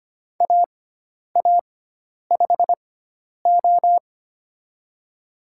Morse code, heard as a keyed tone, about 25 wpm, spaced as for 10 wpm, 700 Hz. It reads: AA5O
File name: Lesson 16g - Letter c - Call Signs 25wpm